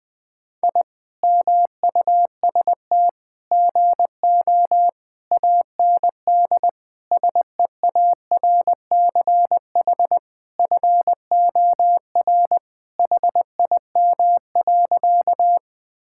Morse code, heard as a keyed tone, about 20 wpm, 700 Hz. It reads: I MUST GO AND SEARCH FOR HIM.
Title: Sentences from Top 1000 Words - Review Part 5 - 20wpm